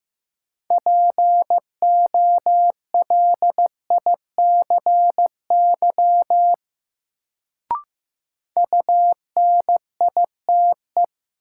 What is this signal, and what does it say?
Morse code, heard as a keyed tone, about 15 wpm, 700 Hz. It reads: POLICY UNITE
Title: Sentences from Top 1000 Words - Review Part 5 - New Word Drill - 15wpm